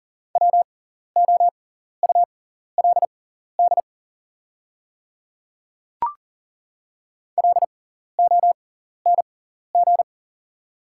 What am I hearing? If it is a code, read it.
Morse code, heard as a keyed tone, about 40 wpm, spaced as for 14 wpm, 690 Hz. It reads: WOULD LONG
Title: 6x Character Spacing - Top 200 Words - 40wpm